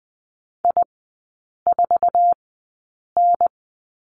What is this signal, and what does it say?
Morse code, heard as a keyed tone, about 20 wpm, spaced as for 8 wpm, 700 Hz. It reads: I4N
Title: 047 Lesson 9e - Number 4 - Three Characters 20wpm